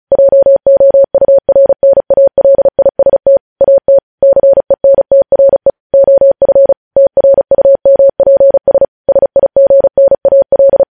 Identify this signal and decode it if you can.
Morse code, heard as a keyed tone, about 36 wpm, 550 Hz. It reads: JOURNALIST AT CENTRE OF TRUMPS SIGNAL
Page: News headlines in Morse code at 35 wpm